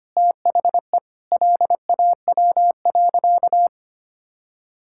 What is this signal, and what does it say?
Morse code, heard as a keyed tone, about 25 wpm, 700 Hz. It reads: THE LAW.
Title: Sentences from Top 1000 Words - Review Part 6 - 25wpm